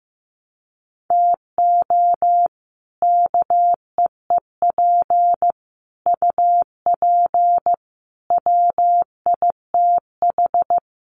Morse code, heard as a keyed tone, about 15 wpm, 700 Hz. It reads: TO KEEP UP WITH
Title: Sets of 5 Words Encore 15wpm